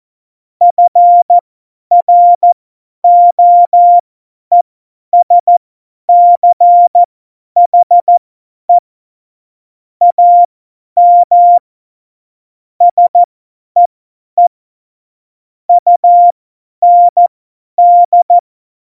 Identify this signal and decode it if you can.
Morse code, heard as a keyed tone, about 14 wpm, spaced as for 10 wpm, 700 Hz. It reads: FROESCHE AM SEE UND
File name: Morsekurs / Witze / Witz 14. August 2018, 50 F70